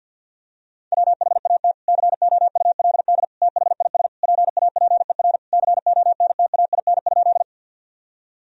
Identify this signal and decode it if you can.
Morse code, heard as a keyed tone, about 52 wpm, 700 Hz. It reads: WHAT COULD THIS PAPER CONTAIN?
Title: Sentences from Top 1000 Words - Review Part 2 - 50wpm